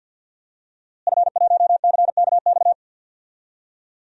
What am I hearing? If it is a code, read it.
Morse code, heard as a keyed tone, about 52 wpm, 700 Hz. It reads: F1CKX
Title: DXCC-100 Call Signs - 5 Characters - Repeated 2x - 50wpm